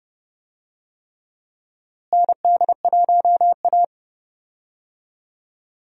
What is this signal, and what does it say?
Morse code, heard as a keyed tone, about 32 wpm, 700 Hz. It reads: ND1A
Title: USA Call Signs - 4 Characters - Repeated 3x - 30wpm